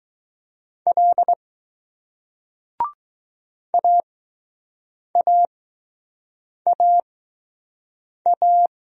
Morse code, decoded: L AAAA

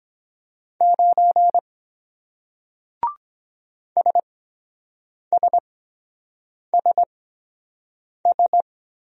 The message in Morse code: 9 SSSS

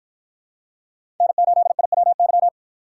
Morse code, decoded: N8IWY

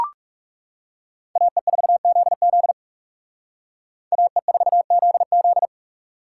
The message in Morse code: AE4ZZ AE4ZZ

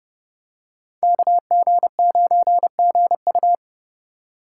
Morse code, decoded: KG9GU

